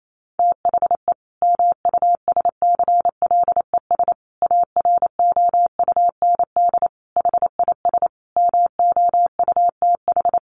Morse code, heard as a keyed tone, about 28 wpm, 700 Hz. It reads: THE MUSCLES AROUND HIS MOUTH
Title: A Sound Of Thunder 28wpm